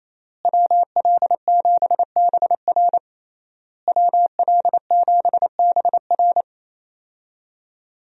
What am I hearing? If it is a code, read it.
Morse code, heard as a keyed tone, about 27 wpm, 700 Hz. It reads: WL7BR WL7BR
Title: USA Call Signs - 5 Characters - Repeated 3x - 28wpm